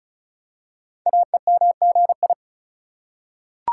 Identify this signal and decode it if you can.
Morse code, heard as a keyed tone, about 36 wpm, 700 Hz. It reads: AEMGI
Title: Code Groups - 5 Letters - 35wpm